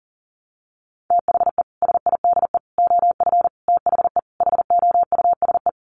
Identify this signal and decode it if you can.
Morse code, heard as a keyed tone, about 40 wpm, 700 Hz. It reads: THE SIDE OF THE HOUSE
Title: Sets of 5 Words Encore2 40wpm